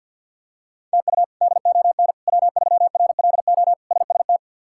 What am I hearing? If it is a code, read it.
Morse code, heard as a keyed tone, about 52 wpm, 700 Hz. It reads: TU DON W3RLO SST
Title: K1USN SST QSO Practice - 50wpm